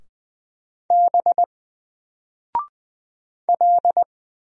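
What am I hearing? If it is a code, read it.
Morse code, heard as a keyed tone, about 20 wpm, 700 Hz. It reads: B L
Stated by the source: Single letter and number Morse drill at 20 wpm, from N4LCD.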